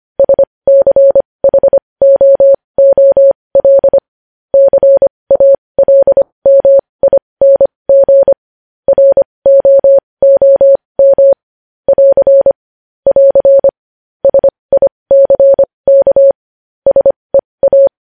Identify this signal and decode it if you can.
Morse code, heard as a keyed tone, about 25 wpm, spaced as for 20 wpm, 550 Hz. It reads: SCHOOL CALMING ROOM <AR> <AR> SICK SEA
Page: news headlines in morse code at 20 wpm